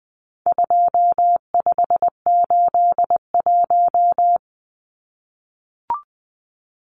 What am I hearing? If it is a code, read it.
Morse code, heard as a keyed tone, about 20 wpm, 700 Hz. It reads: 2581
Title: Numbers - 4-Digit 20wpm